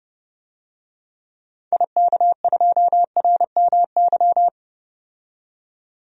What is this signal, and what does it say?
Morse code, heard as a keyed tone, about 30 wpm, 700 Hz. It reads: IK2RMY
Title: DXCC-100 Call Signs - 6 Characters - 30wpm